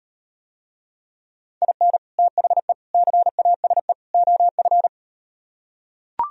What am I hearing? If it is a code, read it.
Morse code, heard as a keyed tone, about 39 wpm, 700 Hz. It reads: IN THE CASE OF